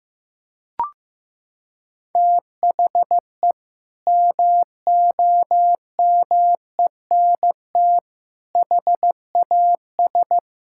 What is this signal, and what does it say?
Morse code, heard as a keyed tone, about 15 wpm, 700 Hz. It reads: THE MOMENT HAS
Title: Sentences from Top 1000 Words - Review Part 5 - 15wpm